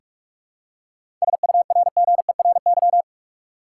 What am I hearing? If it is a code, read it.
Morse code, heard as a keyed tone, about 42 wpm, 700 Hz. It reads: SURGERY
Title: Seven Letter Words - 45wpm